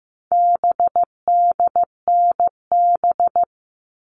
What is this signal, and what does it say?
Morse code, heard as a keyed tone, about 15 wpm, 700 Hz. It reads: BDNB
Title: All Cut Numbers - 4-Digit 15wpm